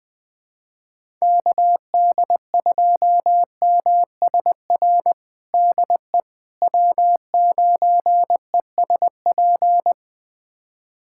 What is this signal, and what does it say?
Morse code, heard as a keyed tone, about 20 wpm, 700 Hz. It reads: KD2MSR DE W9ESP
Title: USA Callsign DE Callsign 20wpm